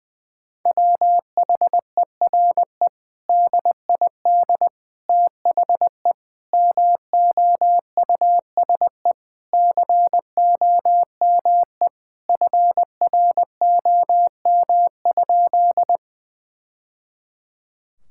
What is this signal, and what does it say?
Morse code, heard as a keyed tone, about 21 wpm, 700 Hz. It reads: WHERE DID THE MOUSE COME FROM?
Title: Sentences from Top 1000 Words - Review Part 5 - 20wpm